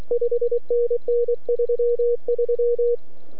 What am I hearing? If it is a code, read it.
Morse code, heard as a keyed tone, about 23 wpm, 480 Hz. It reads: 5NN 33